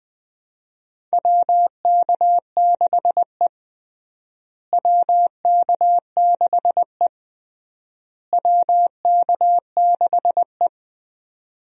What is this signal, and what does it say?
Morse code, heard as a keyed tone, about 20 wpm, 700 Hz. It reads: WK6E WK6E WK6E
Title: USA Call Signs - 4 Characters - Repeated 3x - 20wpm